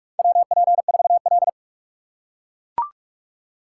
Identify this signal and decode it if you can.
Morse code, heard as a keyed tone, about 42 wpm, 700 Hz. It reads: WP4L
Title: USA Call Signs - 4 Characters - 6x Character Spacing - 45wpm